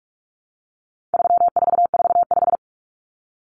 Morse code, heard as a keyed tone, about 44 wpm, 700 Hz. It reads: 3445
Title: Numbers - 4-Digit 45wpm